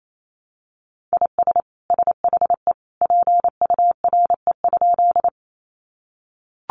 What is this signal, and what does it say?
Morse code, heard as a keyed tone, about 28 wpm, 700 Hz. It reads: IS SHE PURE?
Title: Sentences from Top 1500 Words 28wpm